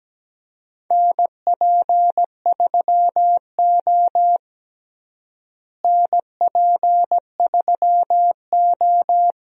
Morse code, heard as a keyed tone, about 17 wpm, 700 Hz. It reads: NP3O NP3O
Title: USA Call Signs - 4 Characters - Repeated 2x - 17wpm